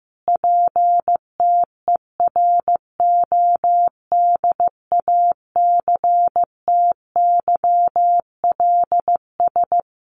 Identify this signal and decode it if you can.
Morse code, heard as a keyed tone, about 15 wpm, 700 Hz. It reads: PTERODACTYLS
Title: A Sound Of Thunder 15wpm